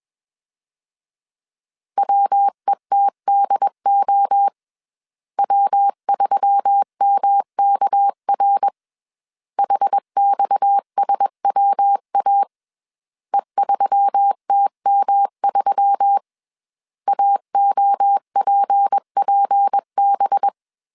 Morse code, heard as a keyed tone, about 21 wpm, 800 Hz. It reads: WETDO W3MKR HXSWA E3TM3 AOPPB
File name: CW / DET / 1 6 1500C S B 20wpm 2048